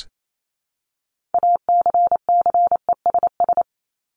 Morse code, heard as a keyed tone, about 28 wpm, 700 Hz. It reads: ACCESS
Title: Top 501-750 Words 28wpm